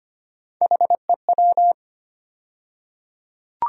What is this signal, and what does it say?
Morse code, heard as a keyed tone, about 25 wpm, 700 Hz. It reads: HEW
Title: USA Call Sign Postfixes - 3 Letters 25wpm